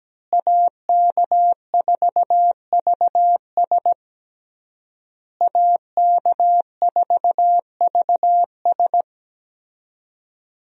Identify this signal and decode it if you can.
Morse code, heard as a keyed tone, about 17 wpm, 700 Hz. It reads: AK4VS AK4VS
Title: USA Call Signs - 5 Characters - Repeated 3x - 17wpm